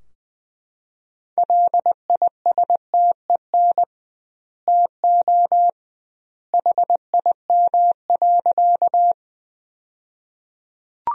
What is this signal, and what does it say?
Morse code, heard as a keyed tone, about 20 wpm, 700 Hz. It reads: LISTEN TO HIM.
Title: 2x Word-Spacing - Sentences from Top 400 Words - 20wpm